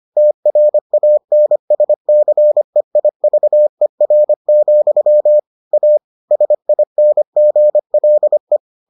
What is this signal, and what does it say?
Morse code, heard as a keyed tone, about 25 wpm, 600 Hz. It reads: TRANSCEIVER, A SINGLE